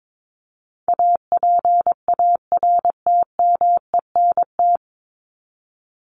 Morse code, heard as a keyed tone, about 22 wpm, 700 Hz. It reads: APARTMENT
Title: Top 751-1000 Words 22wpm